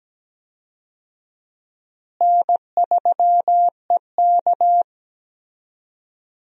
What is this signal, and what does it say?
Morse code, heard as a keyed tone, about 17 wpm, 700 Hz. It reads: N3EK
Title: USA Call Signs - 4 Characters - Repeated 2x - 17wpm